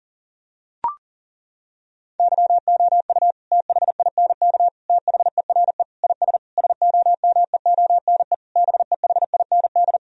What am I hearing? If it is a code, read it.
Morse code, heard as a keyed tone, about 40 wpm, 700 Hz. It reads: YOU THINK THERE IS SOMEONE BEHIND
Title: Sentences from Top 1000 Words - Review Part 4 - 40wpm